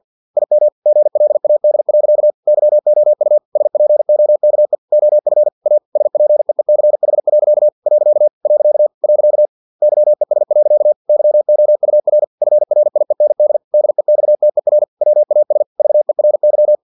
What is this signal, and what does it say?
Morse code, heard as a keyed tone, about 50 wpm, 600 Hz. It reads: IM GLAD. YOU SPOKE OF A SPEECH. . YES. YOUR FRIEND DEXTER WAS VERY
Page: Never 49wpm